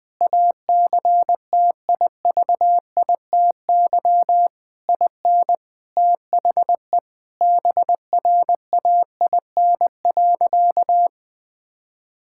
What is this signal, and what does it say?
Morse code, heard as a keyed tone, about 20 wpm, 700 Hz. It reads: ACTIVITY IN THE BRAIN.
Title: Sentences from Top 1000 Words - Review Part 6 - 20wpm